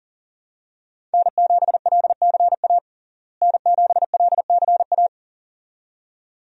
Morse code, heard as a keyed tone, about 37 wpm, 700 Hz. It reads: N7LCA N7LCA
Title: USA Call Signs - 5 Characters - Repeated 2x - 40wpm